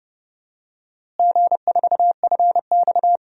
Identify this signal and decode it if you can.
Morse code, heard as a keyed tone, about 30 wpm, 700 Hz. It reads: G4FX
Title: DXCC-100 Call Signs - 4 Characters - Repeated 3x - 30wpm